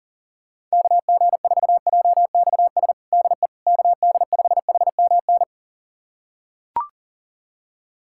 Morse code, heard as a keyed tone, about 40 wpm, 700 Hz. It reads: KG4JXS DE KD5HMN
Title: USA Callsign DE Callsign - 6 Characters - 40wpm